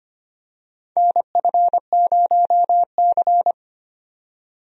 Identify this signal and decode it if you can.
Morse code, heard as a keyed tone, about 25 wpm, 700 Hz. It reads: NF0C
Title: USA Call Signs - 4 Characters - 25wpm